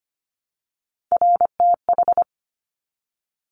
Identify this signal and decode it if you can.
Morse code, heard as a keyed tone, about 25 wpm, 700 Hz. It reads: RTH
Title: Top Three Letter Combinations in Words 25wpm